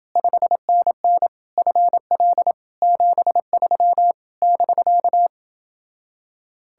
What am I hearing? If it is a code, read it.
Morse code, heard as a keyed tone, about 29 wpm, 700 Hz. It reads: 5NN FL 73 <BK>